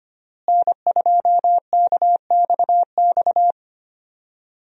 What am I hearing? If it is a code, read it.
Morse code, heard as a keyed tone, about 25 wpm, 700 Hz. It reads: N2KXX
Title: USA Call Signs - 5 Characters - Repeated 3x - 25wpm